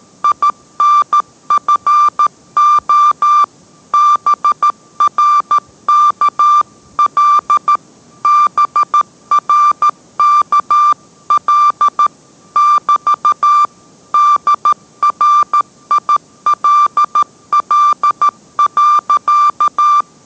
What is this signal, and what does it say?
Morse code, heard as a keyed tone, about 13 wpm, 1180 Hz. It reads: INFO BRKL BRKL <BT> DRILL.